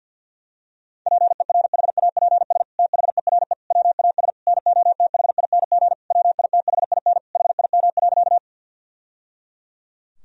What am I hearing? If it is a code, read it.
Morse code, heard as a keyed tone, about 50 wpm, 700 Hz. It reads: PERHAPS THERE WAS NOTHING WITHIN HIM.
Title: Sentences from Top 1000 Words - Review Part 4 - 50wpm